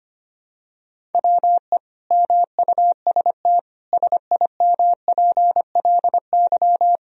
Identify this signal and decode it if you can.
Morse code, heard as a keyed tone, about 25 wpm, 700 Hz. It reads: WE MUST SIMPLY
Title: Sentences from Top 1000 Words - Review Part 4 - 25wpm